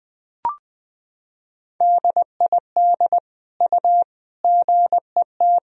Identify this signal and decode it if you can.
Morse code, heard as a keyed tone, about 20 wpm, 700 Hz. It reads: DID U GET
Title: Sentences from Top 400 Words and Abbreviations 20wpm